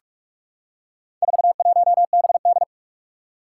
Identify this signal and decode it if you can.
Morse code, heard as a keyed tone, about 44 wpm, 700 Hz. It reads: 41BD
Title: Four Challenging Letter-Number Combinations 45wpm